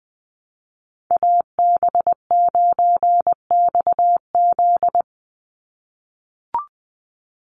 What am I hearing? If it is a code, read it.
Morse code, heard as a keyed tone, about 20 wpm, 700 Hz. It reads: AB9XZ
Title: USA Call Signs 20wpm